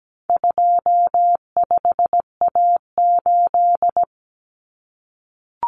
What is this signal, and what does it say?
Morse code, heard as a keyed tone, about 17 wpm, 700 Hz. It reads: 25A8
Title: Common Cut Numbers - 4-Digit 17wpm